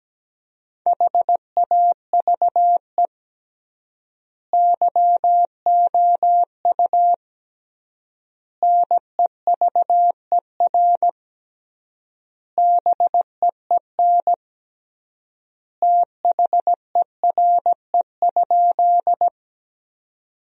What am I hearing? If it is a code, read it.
Morse code, heard as a keyed tone, about 17 wpm, 700 Hz. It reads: HAVE YOU NEVER BEEN THERE?
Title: 3x Word-Spacing - Sentences from Top 300 Words - 17wpm